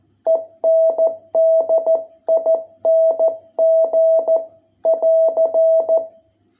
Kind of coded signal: Morse code